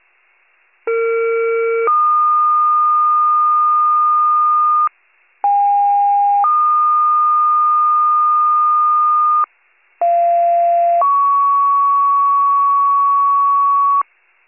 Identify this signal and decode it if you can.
Two-tone page, heard as a one-second tone, then a three-second tone, about 460 then 1220 Hz, about 810 then 1220 Hz, about 690 then 1120 Hz. It is heard from 0.9 to 14.0 s.